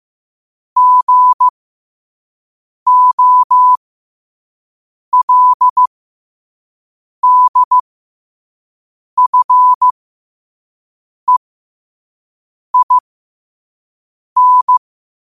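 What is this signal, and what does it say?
Morse code, heard as a keyed tone, about 15 wpm, spaced as for 5 wpm, 1000 Hz. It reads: GOLDFEIN
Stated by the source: Morse code quote of the day at 5 wpm.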